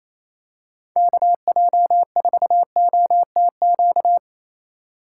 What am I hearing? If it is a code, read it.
Morse code, heard as a keyed tone, about 28 wpm, 700 Hz. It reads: KJ4OTQ
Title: USA Call Signs - 6 Characters - 8x Character Spacing - 28wpm